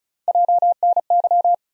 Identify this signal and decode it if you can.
Morse code, heard as a keyed tone, about 34 wpm, 700 Hz. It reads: JNY
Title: USA Call Sign Postfixes - 3 Letters 35wpm